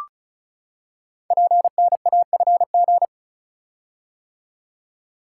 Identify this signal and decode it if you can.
Morse code, heard as a keyed tone, about 36 wpm, 700 Hz. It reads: PNAFG
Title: Code Groups - 5 Letters - 35wpm